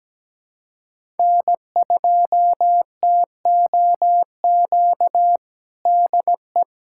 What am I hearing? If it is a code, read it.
Morse code, heard as a keyed tone, about 17 wpm, 700 Hz. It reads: N2TOQ DE
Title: USA Callsign DE Callsign - 5 Characters - 17wpm